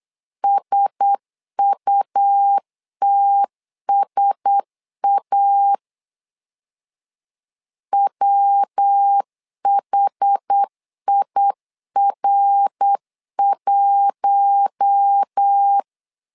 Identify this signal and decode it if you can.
Morse code, heard as a keyed tone, about 8 wpm, 800 Hz. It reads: SUTSA WHIR1
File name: LEZIONI / 4 Q1S 8wpm 2710